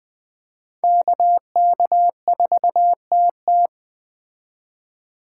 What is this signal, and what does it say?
Morse code, heard as a keyed tone, about 20 wpm, 700 Hz. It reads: KK4TT
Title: Speed Racing - USA 2x2 Call Signs 17wpm